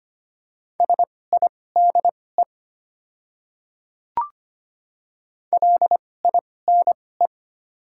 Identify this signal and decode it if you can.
Morse code, heard as a keyed tone, about 25 wpm, spaced as for 18 wpm, 700 Hz. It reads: SIDE LINE
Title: 2x Character Spacing - Top 200 Words - 25wpm